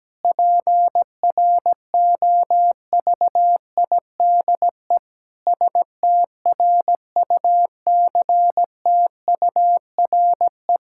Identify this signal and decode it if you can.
Morse code, heard as a keyed tone, about 17 wpm, 700 Hz. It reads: PROVIDE STRUCTURE
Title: Sentences from Top 1000 Words - Review Part 6 - 17wpm